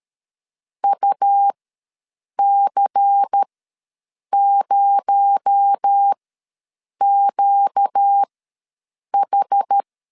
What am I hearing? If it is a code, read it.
Morse code, heard as a keyed tone, about 13 wpm, spaced as for 7 wpm, 800 Hz. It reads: UC0QH
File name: LEZIONI / 6 PKX0 / 6 PKX0 ws 12wpm 3448